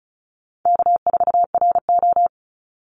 Morse code, heard as a keyed tone, about 34 wpm, 700 Hz. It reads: K4RO
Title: Easy Call Signs Repeated 2x - 35wpm